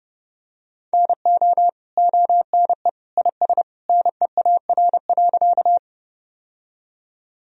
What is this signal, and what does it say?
Morse code, heard as a keyed tone, about 30 wpm, 700 Hz. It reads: NO ONE IS NEAR.
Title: Sentences from Top 1000 Words - Review Part 1 - 30wpm